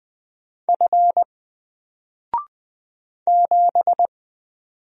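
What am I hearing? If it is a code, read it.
Morse code, heard as a keyed tone, about 20 wpm, 700 Hz. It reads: F 7
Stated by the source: Single letter and number Morse drill at 20 wpm, from N4LCD.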